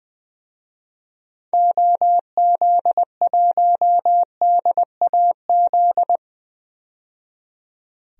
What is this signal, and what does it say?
Morse code, heard as a keyed tone, about 20 wpm, 700 Hz. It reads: OZ1DAZ
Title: DXCC-100 Call Signs - 6 Characters - 20wpm